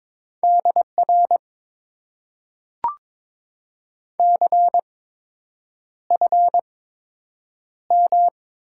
Morse code, heard as a keyed tone, about 22 wpm, spaced as for 8 wpm, 700 Hz. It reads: DR CFM
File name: Top 90 QSO Elements - 8x Character Spacing - 22wpm